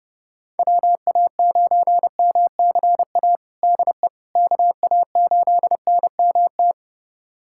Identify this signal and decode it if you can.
Morse code, heard as a keyed tone, about 30 wpm, 700 Hz. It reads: WA9MCA DE KA8NMT
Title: USA Callsign DE Callsign - 6 Characters - 30wpm